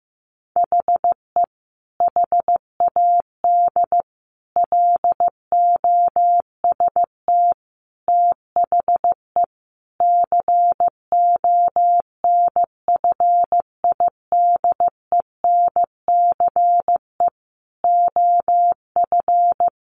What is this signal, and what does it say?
Morse code, heard as a keyed tone, about 15 wpm, 700 Hz. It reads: HE HAD LOST THE CONFIDENCE OF
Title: Sentences from Top 1500 Words 15wpm